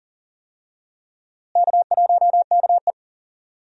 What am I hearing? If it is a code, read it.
Morse code, heard as a keyed tone, about 40 wpm, 700 Hz. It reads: K1KE
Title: USA Call Signs - 4 Characters - Repeated 3x - 40wpm